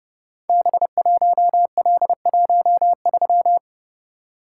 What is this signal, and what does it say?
Morse code, heard as a keyed tone, about 28 wpm, 700 Hz. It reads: B1L13
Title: Code Groups - 5 Letter:Numbers - 30wpm